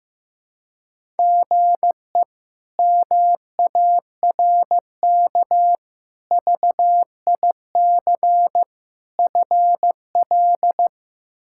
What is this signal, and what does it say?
Morse code, heard as a keyed tone, about 15 wpm, 700 Hz. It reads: GE MARK VIC FL
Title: K1USN SST QSO Practice - 15wpm